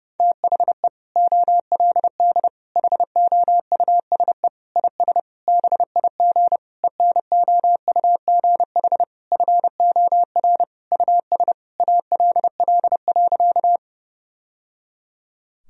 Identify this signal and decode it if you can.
Morse code, heard as a keyed tone, about 30 wpm, 700 Hz. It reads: THE OLD HOUSE IS BIG ENOUGH FOR US ALL.
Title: Sentences from Top 1000 Words - Review Part 1 - 30wpm